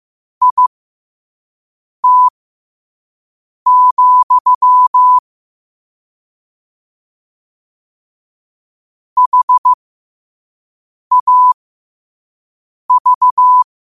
Morse code, heard as a keyed tone, about 15 wpm, spaced as for 5 wpm, 1000 Hz. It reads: IT, HAV
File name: qotdfiles / 5wpm